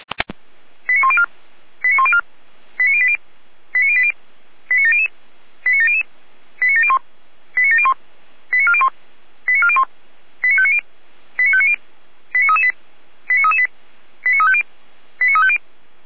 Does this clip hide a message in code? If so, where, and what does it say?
Yes: five-tone selective call (ZVEI) from 0.9 to 15.6 s, heard as quick runs of five tones, tones of about 80 ms. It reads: 89104 89104 89080 89080 89700 89700 89701 89701 89401 89401 89590 89590 89338 89338 89350 89350